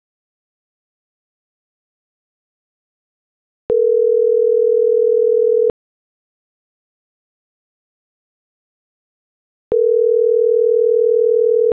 Ringback tone (North America), 440 + 480 Hz.